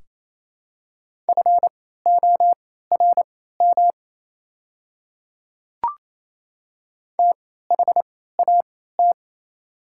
Morse code, heard as a keyed tone, about 28 wpm, spaced as for 16 wpm, 700 Hz. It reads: FORM THAT
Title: 3x Character Spacing - Top 300 Words - 28wpm